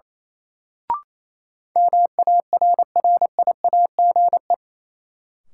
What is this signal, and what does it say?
Morse code, heard as a keyed tone, about 28 wpm, 700 Hz. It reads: MARRIAGE
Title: Sentences from Top 800 Words - New Word Drill - 28wpm